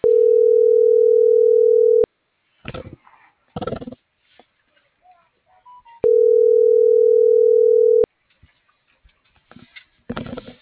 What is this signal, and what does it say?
Ringback tone (North America), 440 + 480 Hz.